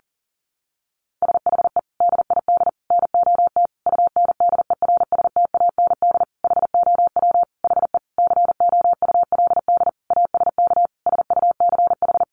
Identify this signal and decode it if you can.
Morse code, heard as a keyed tone, about 40 wpm, 700 Hz. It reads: SHE DID NOT UNDERSTAND HOW HE COULD ASK SUCH
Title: Sentences from Top 600 Words 40wpm